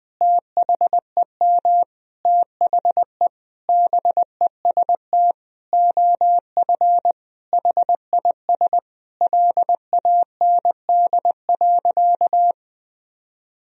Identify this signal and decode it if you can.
Morse code, heard as a keyed tone, about 19 wpm, 700 Hz. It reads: THEM THE BEST OF HIS LAND.
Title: Sentences from Top 1000 Words - Review Part 1 - 20wpm